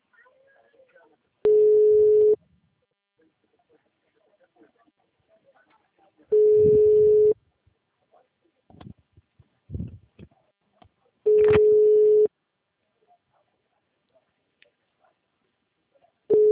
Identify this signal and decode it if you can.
Ringback tone (Europe, 425 Hz), with speech.
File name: out- -2005- - - .56900